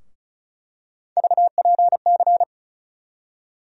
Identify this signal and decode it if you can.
Morse code, heard as a keyed tone, about 36 wpm, 700 Hz. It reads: VPC
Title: USA Call Sign Postfixes - 3 Letters 35wpm